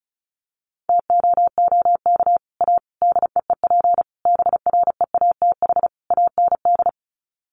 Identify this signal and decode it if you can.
Morse code, heard as a keyed tone, about 34 wpm, 700 Hz. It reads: TOOK A DEEP BREATH AND
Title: Sets of 5 Words Encore 35wpm